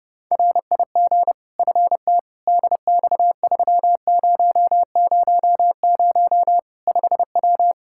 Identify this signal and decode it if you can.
Morse code, heard as a keyed tone, about 32 wpm, 700 Hz. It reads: RIG FT DX3000 5W
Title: Practice QSO - Exchange 2 - Info - 30wpm